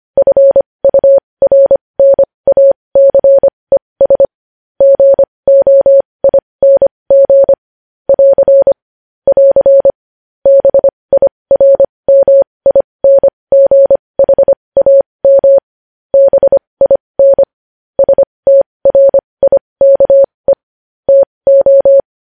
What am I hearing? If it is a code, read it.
Morse code, heard as a keyed tone, about 25 wpm, spaced as for 20 wpm, 550 Hz. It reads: FURNACES GOING <AR> <AR> BIRMINGHAM BIN STRIKE TO
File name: podcast / News headlines-1x-25x20